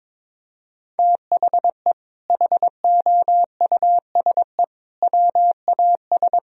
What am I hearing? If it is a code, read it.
Morse code, heard as a keyed tone, about 22 wpm, 700 Hz. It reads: THE HOUSE WAS
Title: Sentences from Top 1000 Words - Review Part 3 - 22wpm